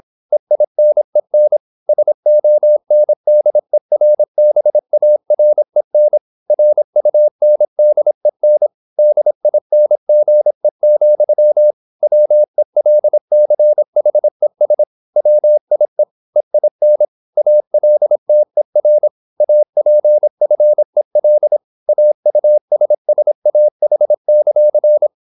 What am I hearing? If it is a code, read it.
Morse code, heard as a keyed tone, about 26 wpm, 600 Hz. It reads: EINEN SONDERBAREN RUNDEN DINGE, WELCHES WIE EIN ALTER APFEL AUSSAH;